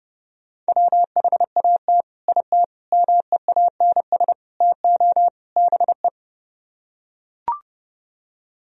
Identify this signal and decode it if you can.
Morse code, heard as a keyed tone, about 30 wpm, 700 Hz. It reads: WHAT IT MEANS TO BE